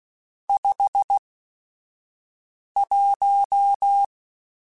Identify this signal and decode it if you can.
Morse code, heard as a keyed tone, about 16 wpm, spaced as for 5 wpm, 800 Hz. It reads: 51
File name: Weather Sets / Weather Sets-mp3s / Weather 4 21.@5